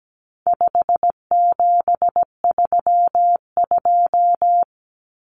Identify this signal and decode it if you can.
Morse code, heard as a keyed tone, about 17 wpm, 700 Hz. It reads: 5732
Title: Numbers - 4-Digit 17wpm